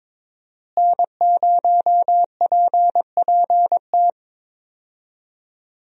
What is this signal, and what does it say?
Morse code, heard as a keyed tone, about 22 wpm, 700 Hz. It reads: N0PPT
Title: USA Call Signs - 5 Characters - Repeated 3x - 22wpm